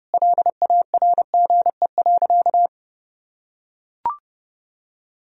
Morse code, heard as a keyed tone, about 30 wpm, 700 Hz. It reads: LARGE.
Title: Sentences from Top 1000 Words - Review Part 1 - 30wpm